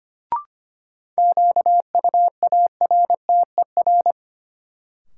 Morse code, heard as a keyed tone, about 25 wpm, 700 Hz. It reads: QUARTER
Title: Sentences from Top 1300 Words - New Word Drill - 25wpm